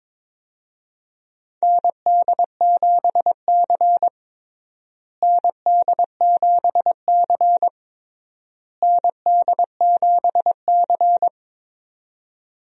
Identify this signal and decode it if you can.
Morse code, heard as a keyed tone, about 22 wpm, 700 Hz. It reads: ND7C ND7C ND7C
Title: USA Call Signs - 4 Characters - Repeated 3x - 22wpm